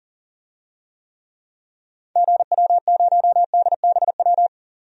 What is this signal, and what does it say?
Morse code, heard as a keyed tone, about 37 wpm, 700 Hz. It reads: GW0DBW
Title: DXCC-100 Call Signs - 6 Characters - 40wpm